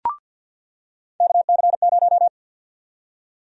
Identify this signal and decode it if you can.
Morse code, heard as a keyed tone, about 48 wpm, 700 Hz. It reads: KC0
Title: USA Call Sign Prefixes - 50wpm